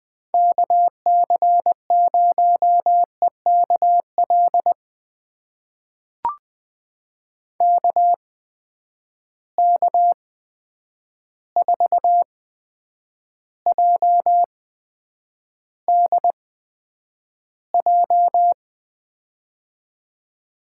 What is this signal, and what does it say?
Morse code, heard as a keyed tone, about 20 wpm, spaced as for 9 wpm, 700 Hz. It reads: KC0EKL KK4JDJ
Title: USA Call Signs - 6 Characters - 8x Character Spacing - 20wpm